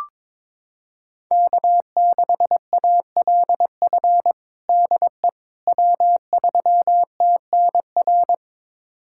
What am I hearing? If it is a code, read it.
Morse code, heard as a keyed tone, about 22 wpm, 700 Hz. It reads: K6ALF DE W3TNR
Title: USA Callsign DE Callsign - 5 Characters - 22wpm